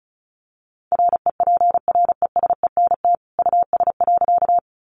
Morse code, heard as a keyed tone, about 34 wpm, 700 Hz. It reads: REPRESENT US.
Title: Sentences from Top 1200 Words 35wpm